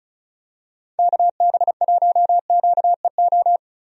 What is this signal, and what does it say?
Morse code, heard as a keyed tone, about 36 wpm, 700 Hz. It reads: KB1QEO
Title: USA Call Signs - 6 Characters - 35wpm